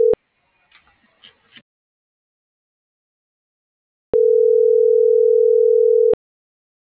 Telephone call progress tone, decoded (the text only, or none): ringback tone